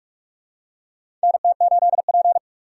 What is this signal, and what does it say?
Morse code, heard as a keyed tone, about 46 wpm, 700 Hz. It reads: NT8P
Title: USA Call Signs - 4 Characters - 6x Character Spacing - 45wpm